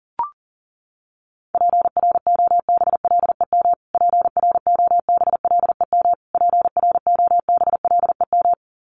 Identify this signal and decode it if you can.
Morse code, heard as a keyed tone, about 40 wpm, 700 Hz. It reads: PROBLEM PROBLEM PROBLEM
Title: Top 500 Words Repeated 40wpm